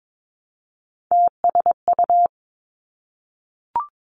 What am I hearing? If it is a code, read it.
Morse code, heard as a keyed tone, about 22 wpm, 700 Hz. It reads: TSU
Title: Three Challenging Letter-Number Combinations 22wpm